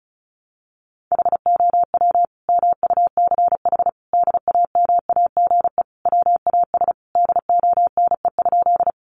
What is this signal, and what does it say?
Morse code, heard as a keyed tone, about 34 wpm, 700 Hz. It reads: HOW MUCH DAMAGE WAS DONE?
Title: Sentences from Top 1100 Words 35wpm